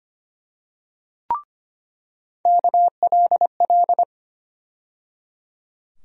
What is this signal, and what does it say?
Morse code, heard as a keyed tone, about 25 wpm, 700 Hz. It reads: KLL
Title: USA Call Sign Postfixes - 3 Letters 25wpm